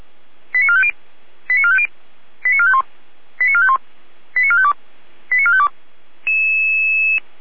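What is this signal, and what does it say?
Five-tone selective call (ZVEI), heard as quick runs of five tones, tones of about 70 ms. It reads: 89450 89450 89451 89451 89452 89452
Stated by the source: ZVEI five-tone call 89451 received by SDR.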